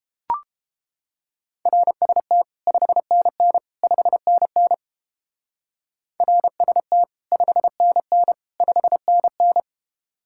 Morse code, heard as a keyed tone, about 29 wpm, 700 Hz. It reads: RST 5NN 5NN RST 5NN 5NN